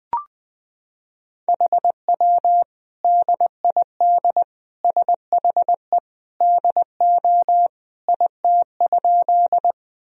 Morse code, heard as a keyed tone, about 20 wpm, 700 Hz. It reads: HW DID SHE DO IT?
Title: Sentences from Top 100 Words and Abbreviations 20wpm